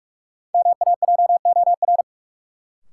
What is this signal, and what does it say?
Morse code, heard as a keyed tone, about 46 wpm, 700 Hz. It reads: MAJOR
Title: Sentences from Top 600 Words - New Word Drill - 45wpm